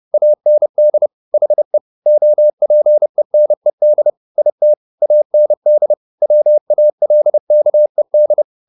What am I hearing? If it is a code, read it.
Morse code, heard as a keyed tone, about 30 wpm, 600 Hz. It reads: AND HE OPENED IT AND WALKED
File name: the old man and the sea / the old man and the sea 1 30wpm